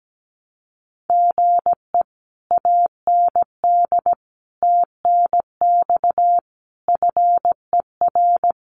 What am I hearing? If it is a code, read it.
Morse code, heard as a keyed tone, about 17 wpm, 700 Hz. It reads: GE AND TNX FER
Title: Common QSO Phrases Encore 17wpm